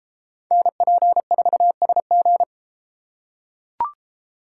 Morse code, heard as a keyed tone, about 33 wpm, 700 Hz. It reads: NP4SG